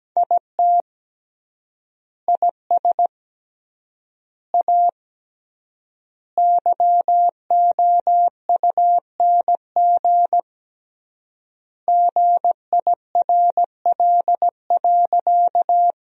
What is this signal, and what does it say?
Morse code, heard as a keyed tone, about 17 wpm, 700 Hz. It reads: IT IS A YOUNG GIRL.
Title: 3x Word-Spacing - Sentences from Top 400 Words - 17wpm